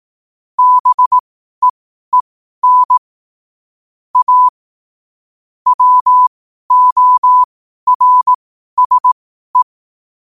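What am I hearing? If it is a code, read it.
Morse code, heard as a keyed tone, about 18 wpm, spaced as for 12 wpm, 1000 Hz. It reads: BEEN A WORSE
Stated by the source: Morse code quotes at 12 words per minute, by ai4qr.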